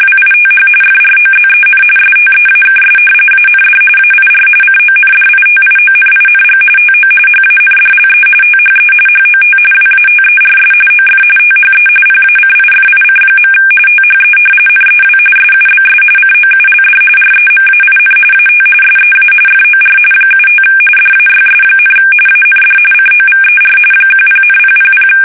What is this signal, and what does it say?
RTTY, about 45.45 baud. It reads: WITH FROZEN STEAKS AND AIR CONDITIONING TO HANDLE THE HOT WATERS OF THE PACIFIC. THE NEW SUBS COULD REMAIN AT SEA FOR 75 DAYS, AND COVER 10,000
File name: fsk850